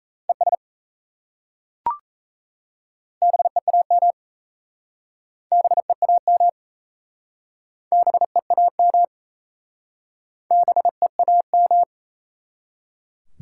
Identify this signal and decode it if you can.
Morse code, heard as a keyed tone, about 37 wpm, 700 Hz. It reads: ES BEAM BEAM BEAM BEAM